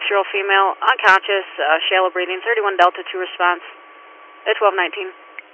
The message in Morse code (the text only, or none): none